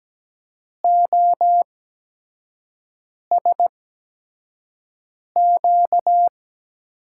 Morse code, heard as a keyed tone, about 17 wpm, spaced as for 5 wpm, 700 Hz. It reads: OSQ
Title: USA Call Signs - 5 Characters - 8x Character Spacing - 17wpm